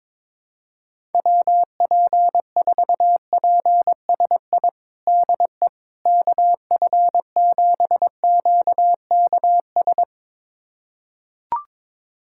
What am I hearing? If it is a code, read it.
Morse code, heard as a keyed tone, about 22 wpm, 700 Hz. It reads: WP4PSI DE KF7QKS
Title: USA Callsign DE Callsign - 6 Characters - 22wpm